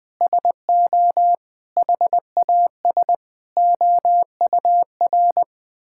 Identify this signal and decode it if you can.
Morse code, heard as a keyed tone, about 20 wpm, 700 Hz. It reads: SO HAS OUR
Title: Sentences with 9-Word Phrases - Spoken Beginning - 20wpm